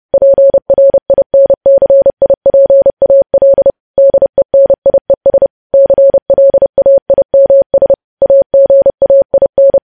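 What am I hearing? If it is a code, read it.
Morse code, heard as a keyed tone, about 30 wpm, 550 Hz. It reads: PRINCIPAL DENIES CLAIMS AGAIN
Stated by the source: Morse code sending of news headlines at 30 wpm.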